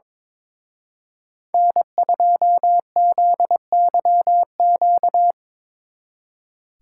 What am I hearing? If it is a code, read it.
Morse code, heard as a keyed tone, about 26 wpm, 700 Hz. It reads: N2ZYQ
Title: Speed Racing - USA 1x3 Call Signs 22wpm